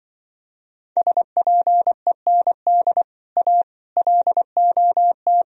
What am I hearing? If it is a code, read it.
Morse code, heard as a keyed tone, about 24 wpm, 700 Hz. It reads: SPEND A LOT
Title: Speed-Racing - Sets of 5 Words 20wpm